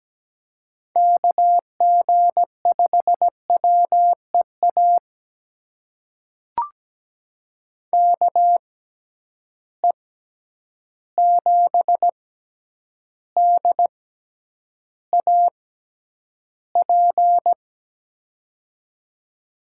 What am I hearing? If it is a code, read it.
Morse code, heard as a keyed tone, about 17 wpm, spaced as for 9 wpm, 700 Hz. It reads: KG5WEA KE7DAP
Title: USA Call Signs - 6 Characters - 6x Character Spacing - 17wpm